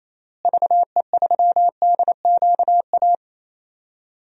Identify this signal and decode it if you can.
Morse code, heard as a keyed tone, about 28 wpm, 700 Hz. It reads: VE3DQA
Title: DXCC-100 Call Signs - 6 Characters - Repeated 3x - 28wpm